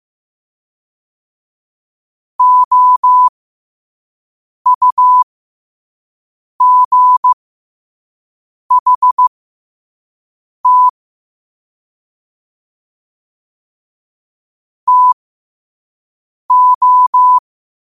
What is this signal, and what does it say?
Morse code, heard as a keyed tone, about 15 wpm, spaced as for 5 wpm, 1000 Hz. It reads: OUGHT TO
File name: qotdfiles / 5wpm